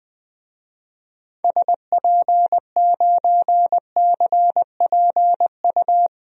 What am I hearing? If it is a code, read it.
Morse code, heard as a keyed tone, about 20 wpm, 700 Hz. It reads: SP9CPU
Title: DXCC-100 Call Signs - 6 Characters - Repeated 3x - 20wpm